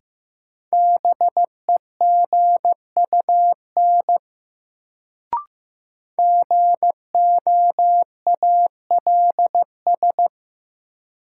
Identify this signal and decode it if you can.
Morse code, heard as a keyed tone, about 15 wpm, 700 Hz. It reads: BEGUN GOALS
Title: Sentences from Top 1000 Words - Review Part 6 - New Word Drill - 15wpm